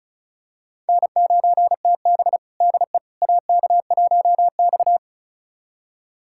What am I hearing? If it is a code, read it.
Morse code, heard as a keyed tone, about 36 wpm, 700 Hz. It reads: N9TB DE AK1X
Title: USA Callsign DE Callsign - 4 Characters - 35wpm